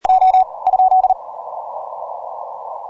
Morse code, heard as a keyed tone, about 40 wpm, 750 Hz. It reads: O ?
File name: simple-file-list / HD8R 15MCW